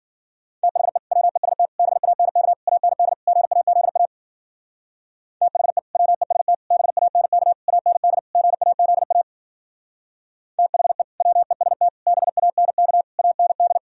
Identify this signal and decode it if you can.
Morse code, heard as a keyed tone, about 50 wpm, 700 Hz. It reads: THE WEST BANK AND GAZA THE WEST BANK AND GAZA THE WEST BANK AND
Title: Speed-Racing - Sets of 5 Words 40wpm